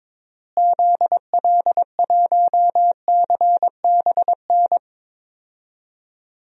Morse code, heard as a keyed tone, about 22 wpm, 700 Hz. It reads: ZL1CBN
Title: DXCC-100 Call Signs - 6 Characters - Repeated 3x - 22wpm